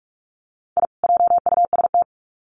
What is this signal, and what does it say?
Morse code, heard as a keyed tone, about 44 wpm, 700 Hz. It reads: I JUST
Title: Sets of 2 Words 45wpm